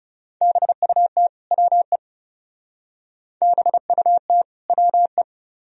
Morse code, heard as a keyed tone, about 35 wpm, 700 Hz. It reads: BUT WE BUT WE
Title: Speed-Racing - Sets of 2 Words 30wpm